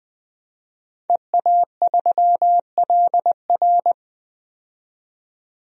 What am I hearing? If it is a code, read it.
Morse code, heard as a keyed tone, about 20 wpm, 700 Hz. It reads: EA3LR
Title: DXCC-100 Call Signs - 5 Characters - 20wpm